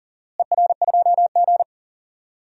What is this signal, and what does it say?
Morse code, heard as a keyed tone, about 39 wpm, 700 Hz. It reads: ER2G
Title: DXCC-100 Call Signs - 4 Characters - Repeated 3x - 40wpm